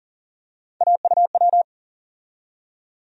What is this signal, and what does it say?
Morse code, heard as a keyed tone, about 39 wpm, 700 Hz. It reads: AUW
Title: USA Call Sign Postfixes - 3 Letters 40wpm